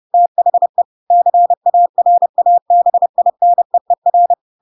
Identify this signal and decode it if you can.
Morse code, heard as a keyed tone, about 30 wpm, 700 Hz. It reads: THE CARABINEER
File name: Pinocchio / Pinocchio ch3 30 30 ik1hge